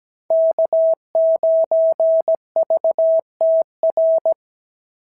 Morse code, heard as a keyed tone, about 16 wpm, 650 Hz. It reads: K9VTR
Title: SOTA QSO Practice 17wpm